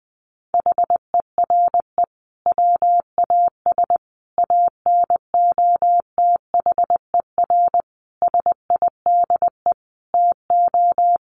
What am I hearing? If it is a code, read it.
Morse code, heard as a keyed tone, about 19 wpm, 700 Hz. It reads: HERE WAS ANOTHER SIDE TO
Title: Sentences from Top 1500 Words 20wpm